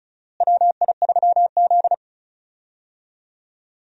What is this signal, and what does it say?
Morse code, heard as a keyed tone, about 36 wpm, 700 Hz. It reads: WI3Z
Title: USA Call Signs - 4 Characters - Repeated 3x - 35wpm